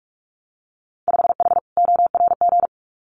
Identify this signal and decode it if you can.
Morse code, heard as a keyed tone, about 46 wpm, 700 Hz. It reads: 5H ORG
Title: Winter Field Day Exchanges Encore 45wpm